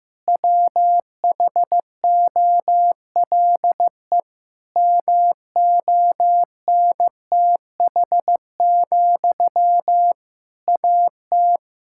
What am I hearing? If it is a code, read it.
Morse code, heard as a keyed tone, about 15 wpm, 700 Hz. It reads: WHOLE MONTH, AT
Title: Sentences from Top 1000 Words - Review Part 5 - 15wpm